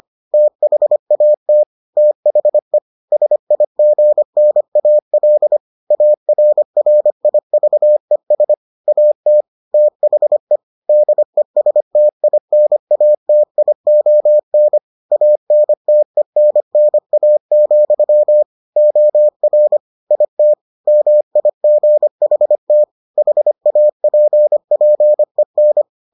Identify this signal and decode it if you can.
Morse code, heard as a keyed tone, about 25 wpm, 600 Hz. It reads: THAT THE SIGNAL ARRIVES AT THE DESTINATION ANTENNA, OR IT MIGHT HAPPEN